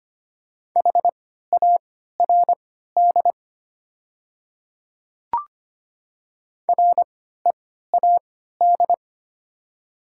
Morse code, heard as a keyed tone, about 25 wpm, spaced as for 14 wpm, 700 Hz. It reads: HARD READ
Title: 3x Character Spacing - Top 500 Words - 25wpm